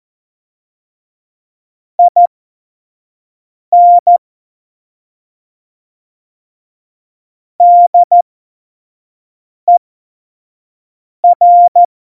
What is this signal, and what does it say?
Morse code, heard as a keyed tone, about 14 wpm, spaced as for 5 wpm, 700 Hz. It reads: IN DER
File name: Morsekurs / Witze / Witz 14. August 2018, 25 F70